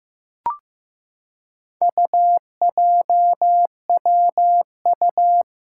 Morse code, heard as a keyed tone, about 15 wpm, 700 Hz. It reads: UJWU
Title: Four Challenging Letter-Number Combinations 15wpm